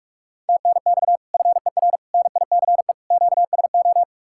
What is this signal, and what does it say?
Morse code, heard as a keyed tone, about 54 wpm, 690 Hz. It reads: TNX FER NICE QSO
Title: Speed-Racing Common QSO Phrases - 45wpm